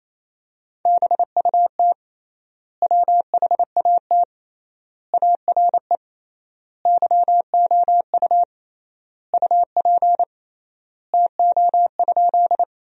Morse code, heard as a keyed tone, about 28 wpm, 700 Hz. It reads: BUT WHAT ARE YOU UP TO?
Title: 3x Word-Spacing - Sentences from Top 100 Words - 28wpm